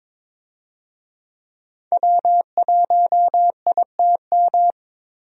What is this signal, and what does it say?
Morse code, heard as a keyed tone, about 22 wpm, 700 Hz. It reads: W1ITM
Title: USA Call Signs - 5 Characters - Repeated 2x - 22wpm